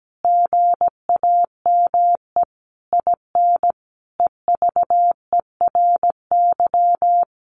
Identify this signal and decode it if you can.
Morse code, heard as a keyed tone, about 17 wpm, 700 Hz. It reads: GAME IN EVERY
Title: Sentences from Top 1500 Words 17wpm